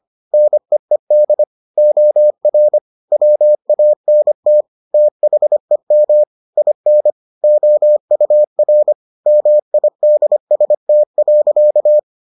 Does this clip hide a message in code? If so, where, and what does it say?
Yes: Morse code from 0.3 to 12.0 s, heard as a keyed tone, about 25 wpm, 600 Hz. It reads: NEED OR WANT THEM IN OUR MIDST.